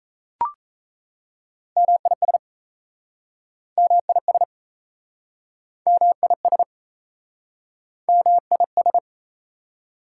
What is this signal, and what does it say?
Morse code, heard as a keyed tone, about 38 wpm, 690 Hz. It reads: MIS MIS MIS MIS